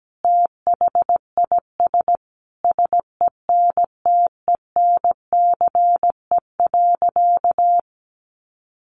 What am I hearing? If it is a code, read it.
Morse code, heard as a keyed tone, about 17 wpm, 700 Hz. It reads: THIS SENTENCE.
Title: Additional Easy Phrases 17wpm